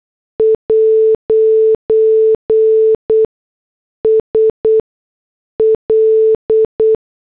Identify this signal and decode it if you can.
Morse code, heard as a keyed tone, about 8 wpm, spaced as for 6 wpm, 440 Hz. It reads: 'S L